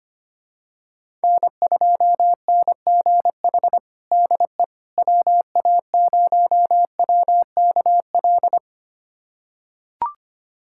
Morse code, heard as a keyed tone, about 25 wpm, 700 Hz. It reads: N2NGH DE WA0WKL
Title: USA Callsign DE Callsign 25wpm